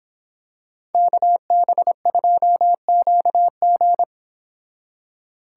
Morse code, heard as a keyed tone, about 26 wpm, 700 Hz. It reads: KB2QG